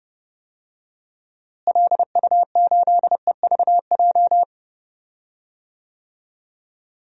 Morse code, heard as a keyed tone, about 30 wpm, 700 Hz. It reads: LU8EVJ
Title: DXCC-100 Call Signs - 6 Characters - Repeated 2x - 30wpm